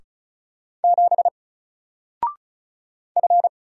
Morse code, heard as a keyed tone, about 33 wpm, 700 Hz. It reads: 7 F
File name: Single-Letter-Number-35wpm